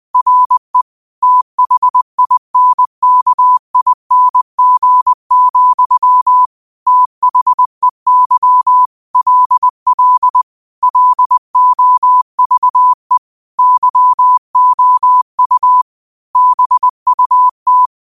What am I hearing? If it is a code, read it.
Morse code, heard as a keyed tone, about 20 wpm, 1000 Hz. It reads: RE THINKING, THEY LL LOVE YOU BUT
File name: qotdfiles / 20wpm